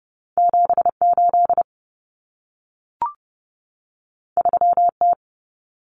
Morse code, heard as a keyed tone, about 30 wpm, 700 Hz. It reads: 78 3T
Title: Common Cut Numbers - 2-Digit 30wpm